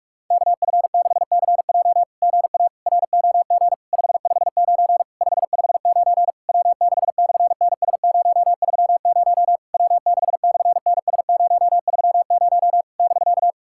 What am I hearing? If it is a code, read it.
Morse code, heard as a keyed tone, about 44 wpm, 690 Hz. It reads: KF6CJ GA ROG 559 559 W6/NS030 W6/NS030 <BK>
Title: SOTA QSO Practice 45wpm